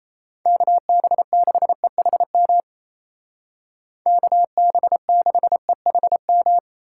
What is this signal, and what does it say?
Morse code, heard as a keyed tone, about 34 wpm, 700 Hz. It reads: KB6EHM KB6EHM
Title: Speed Racing - USA 2x3 Call Signs 28wpm